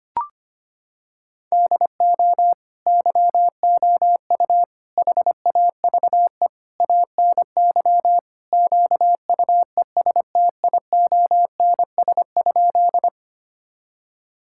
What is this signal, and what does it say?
Morse code, heard as a keyed tone, about 25 wpm, 700 Hz. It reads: DO YOU HAVE ANY QUESTIONS?
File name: practice / 25wpm / Sentences with 4-Word Phrases - 25wpm